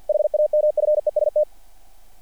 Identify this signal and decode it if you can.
Morse code, heard as a keyed tone, about 48 wpm, 690 Hz. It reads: HAMFEST
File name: archived / CRSnet